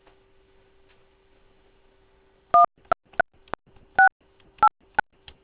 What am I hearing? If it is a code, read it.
DTMF, heard as telephone touch tones, tones of about 120 ms. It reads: 1 68
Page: DAHDI channels send shortened duration DTMF tones.